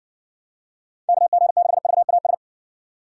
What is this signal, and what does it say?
Morse code, heard as a keyed tone, about 55 wpm, 700 Hz. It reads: KG6VAS